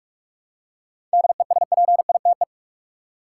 Morse code, heard as a keyed tone, about 46 wpm, 700 Hz. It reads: DESPITE
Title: Seven Letter Words - 45wpm